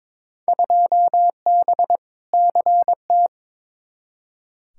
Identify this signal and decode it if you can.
Morse code, heard as a keyed tone, about 22 wpm, 700 Hz. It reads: 2B CT
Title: Field Day Exchanges Encore - 22wpm